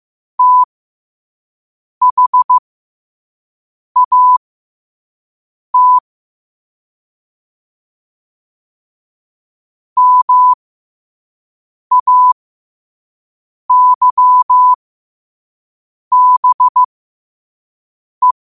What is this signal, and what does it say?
Morse code, heard as a keyed tone, about 15 wpm, spaced as for 5 wpm, 1000 Hz. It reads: THAT MAYBE
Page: learn-to-code